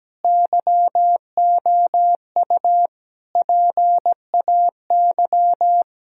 Morse code, heard as a keyed tone, about 17 wpm, 700 Hz. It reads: YOU PAY
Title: Sentences from Top 1000 Words - Review Part 3 - 17wpm